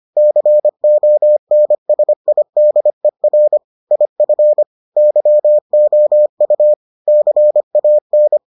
Morse code, heard as a keyed tone, about 25 wpm, 600 Hz. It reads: CONSIDER IF YOU CAN